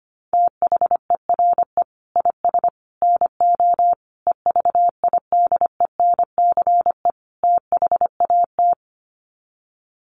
Morse code, heard as a keyed tone, about 25 wpm, 700 Hz. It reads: THERE IS NO EVIDENCE THAT
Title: Sets of 5 Words Encore 25wpm